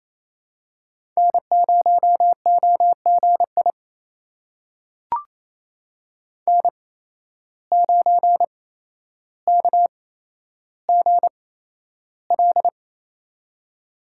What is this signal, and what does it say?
Morse code, heard as a keyed tone, about 27 wpm, spaced as for 12 wpm, 700 Hz. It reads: N0OGI N9KGL
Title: USA Call Signs - 5 Characters - 8x Character Spacing - 28wpm